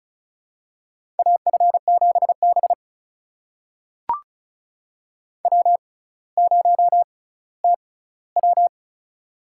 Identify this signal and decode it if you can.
Morse code, heard as a keyed tone, about 36 wpm, spaced as for 18 wpm, 700 Hz. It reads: AF7B W0TW
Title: USA Call Signs - 6x Character Spacing - 35wpm